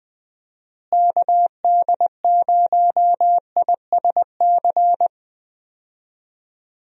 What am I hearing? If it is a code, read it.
Morse code, heard as a keyed tone, about 20 wpm, 700 Hz. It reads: KD0ISC
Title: USA Call Signs - 6 Characters - Repeated 3x - 20wpm